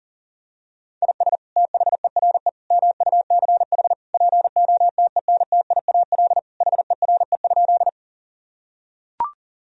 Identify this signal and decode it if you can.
Morse code, heard as a keyed tone, about 40 wpm, 700 Hz. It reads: IS THERE MUCH POTENTIAL HERE?
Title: Sentences from Top 1000 Words - Review Part 4 - 40wpm